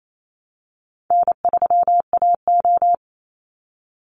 Morse code, heard as a keyed tone, about 28 wpm, 700 Hz. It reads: N3AO
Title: Easy Call Signs Repeated 3x - 28wpm